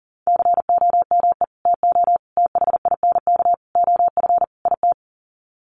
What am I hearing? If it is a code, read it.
Morse code, heard as a keyed tone, about 39 wpm, 700 Hz. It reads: COME TO THINK OF IT
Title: Sets of 5 Words Encore 40wpm